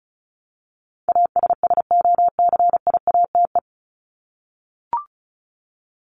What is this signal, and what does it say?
Morse code, heard as a keyed tone, about 34 wpm, 700 Hz. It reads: ASSOCIATE
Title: Top 1751-2000 Words 35wpm